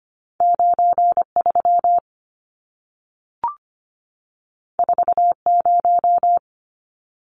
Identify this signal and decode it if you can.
Morse code, heard as a keyed tone, about 25 wpm, 700 Hz. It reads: 93 40
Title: Numbers - 2-Digit 25wpm